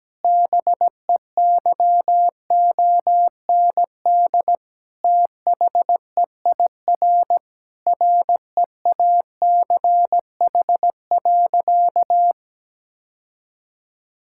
Morse code, heard as a keyed tone, about 17 wpm, 700 Hz. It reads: BEYOND THEIR REACH.
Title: Sentences from Top 1000 Words - Review Part 6 - 17wpm